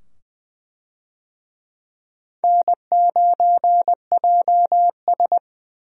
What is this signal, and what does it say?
Morse code, heard as a keyed tone, about 20 wpm, 700 Hz. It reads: N9JS
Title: USA Call Signs - 4 Characters - 20wpm